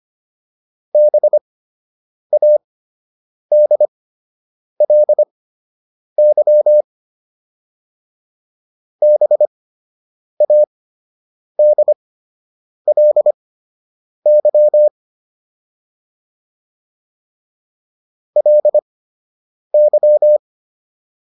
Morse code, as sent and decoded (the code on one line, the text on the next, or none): -... .- -.. .-.. -.-- / -... .- -.. .-.. -.-- / .-.. -.--
BADLY BADLY LY